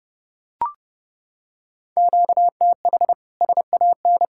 Morse code, heard as a keyed tone, about 29 wpm, 700 Hz. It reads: QTH SAN